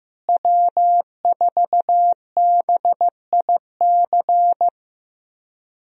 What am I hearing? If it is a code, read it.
Morse code, heard as a keyed tone, about 15 wpm, 700 Hz. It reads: W4BIC
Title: USA Call Signs - 5 Characters - Repeated 3x - 15wpm